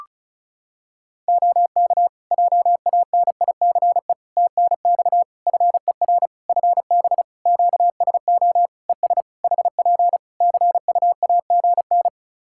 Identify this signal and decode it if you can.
Morse code, heard as a keyed tone, about 36 wpm, 700 Hz. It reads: OK JANICE TNX FER FB QSO ES HP CUAGN
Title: K1USN SST QSO Practice 35wpm